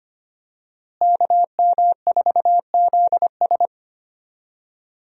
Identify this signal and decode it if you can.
Morse code, heard as a keyed tone, about 25 wpm, 700 Hz. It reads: KM4ZS
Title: USA Call Signs - 5 Characters - Repeated 2x - 25wpm